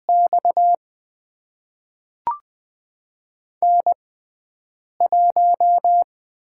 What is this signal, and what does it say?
Morse code, heard as a keyed tone, about 20 wpm, spaced as for 7 wpm, 700 Hz. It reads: X N1
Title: USA Call Signs - 6x Character Spacing - 20wpm